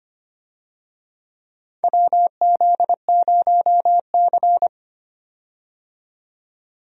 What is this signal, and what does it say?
Morse code, heard as a keyed tone, about 25 wpm, 700 Hz. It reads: WZ0C
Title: USA Call Signs - 4 Characters - 25wpm